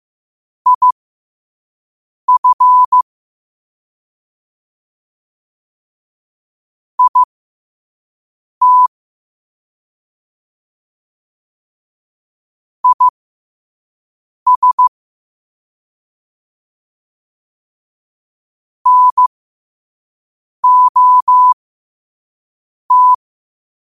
Morse code, heard as a keyed tone, about 15 wpm, spaced as for 5 wpm, 1000 Hz. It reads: IF IT IS NOT
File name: qotdfiles / 5wpm